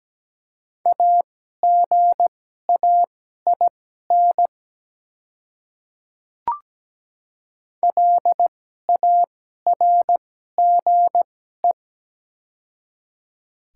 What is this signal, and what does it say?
Morse code, heard as a keyed tone, about 17 wpm, spaced as for 12 wpm, 700 Hz. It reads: AGAIN LARGE
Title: 2x Character Spacing - Top 200 Words - 17wpm